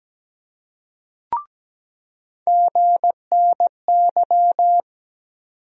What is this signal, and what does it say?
Morse code, heard as a keyed tone, about 17 wpm, 700 Hz. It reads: GNY
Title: USA Call Sign Postfixes - 3 Letters 17wpm